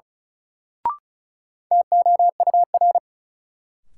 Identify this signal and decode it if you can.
Morse code, heard as a keyed tone, about 36 wpm, 700 Hz. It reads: TOUR
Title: Sentences from Top 1400 Words - New Word Drill Alt - 35wpm